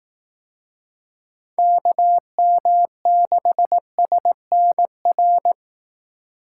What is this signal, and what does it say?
Morse code, heard as a keyed tone, about 18 wpm, 700 Hz. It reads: KM6SNR
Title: Speed Racing - USA 2x3 Call Signs 15wpm